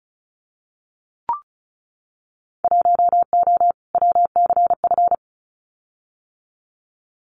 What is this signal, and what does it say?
Morse code, heard as a keyed tone, about 34 wpm, 690 Hz. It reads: 1O WCF
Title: Winter Field Day Exchanges 35wpm